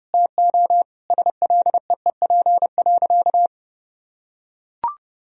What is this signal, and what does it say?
Morse code, heard as a keyed tone, about 32 wpm, 700 Hz. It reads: TO SLEEP.
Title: Sentences from Top 1000 Words - Review Part 2 - 30wpm